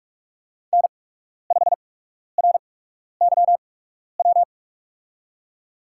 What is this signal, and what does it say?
Morse code, heard as a keyed tone, about 43 wpm, spaced as for 12 wpm, 700 Hz. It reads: N5RYW
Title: USA Call Signs - 5 Characters - 8x Character Spacing - 45wpm